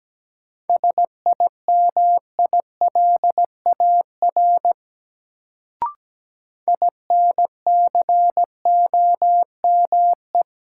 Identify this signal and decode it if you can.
Morse code, heard as a keyed tone, about 17 wpm, 700 Hz. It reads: SIMILAR INCOME
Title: Sentences from Top 1000 Words - Review Part 5 - New Word Drill - 17wpm